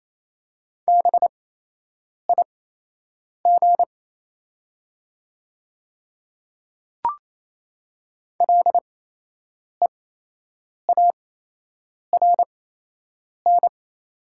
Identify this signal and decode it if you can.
Morse code, heard as a keyed tone, about 28 wpm, spaced as for 8 wpm, 700 Hz. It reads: BIG LEARN
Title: 8x Character Spacing - Top 300 Words - 28wpm